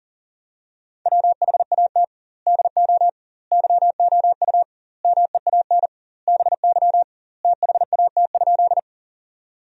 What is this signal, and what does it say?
Morse code, heard as a keyed tone, about 40 wpm, 700 Hz. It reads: WHAT DO YOU MEAN BY THAT?
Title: 2x Word-Spacing - Sentences from Top 200 Words - 40wpm